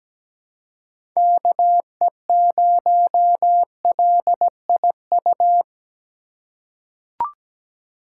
Morse code, heard as a keyed tone, about 17 wpm, 700 Hz. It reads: KE0LIU
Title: USA Call Signs - 3x Character Spacing - 17wpm